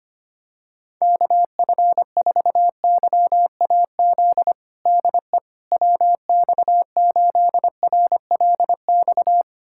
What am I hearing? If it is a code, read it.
Morse code, heard as a keyed tone, about 25 wpm, 700 Hz. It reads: KF4YAZ DE WX8RLX
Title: USA Callsign DE Callsign 25wpm